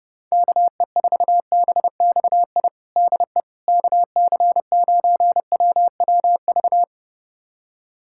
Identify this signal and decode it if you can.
Morse code, heard as a keyed tone, about 30 wpm, 700 Hz. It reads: KE4BXI DE KC9WWV
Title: USA Callsign DE Callsign 30wpm